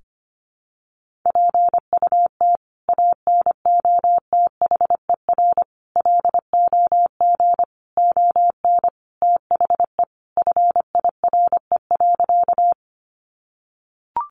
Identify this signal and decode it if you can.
Morse code, heard as a keyed tone, about 25 wpm, 700 Hz. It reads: PUT ANOTHER LOG ON THE FIRE.
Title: Sentences from Top 1400 Words 25wpm